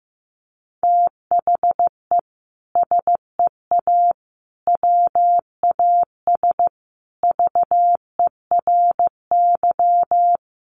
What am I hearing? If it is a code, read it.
Morse code, heard as a keyed tone, about 15 wpm, 700 Hz. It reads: THE SEA WAS VERY
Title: Sentences from Top 1400 Words 15wpm